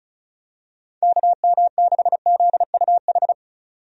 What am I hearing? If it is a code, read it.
Morse code, heard as a keyed tone, about 36 wpm, 700 Hz. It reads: KM6ZUH
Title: USA Call Signs - 6 Characters - 35wpm